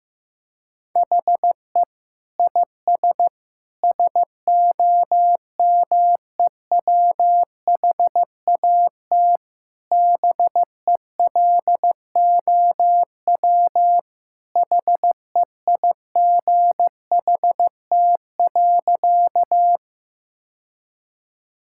Morse code, heard as a keyed tone, about 15 wpm, 700 Hz. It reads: HE IS SOMEWHAT BELOW HEIGHT.
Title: Sentences from Top 1600 Words 15wpm